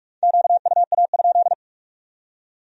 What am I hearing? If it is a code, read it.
Morse code, heard as a keyed tone, about 54 wpm, 700 Hz. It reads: QUA?
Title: Speed-Racing Q-Codes - 45wpm